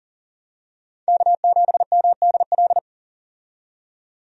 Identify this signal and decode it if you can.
Morse code, heard as a keyed tone, about 38 wpm, 700 Hz. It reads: K7MDL
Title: USA Call Signs - 5 Characters - 40wpm